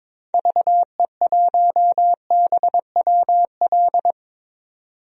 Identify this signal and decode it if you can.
Morse code, heard as a keyed tone, about 22 wpm, 700 Hz. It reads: VE1BWL
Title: DXCC-100 Call Signs - 6 Characters - 22wpm